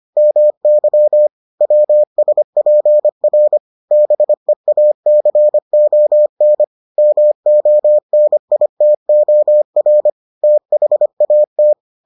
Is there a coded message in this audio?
Yes: Morse code from 0.2 to 11.7 s, heard as a keyed tone, about 25 wpm, 600 Hz. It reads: MY WSPR BEACON MONITOR THAT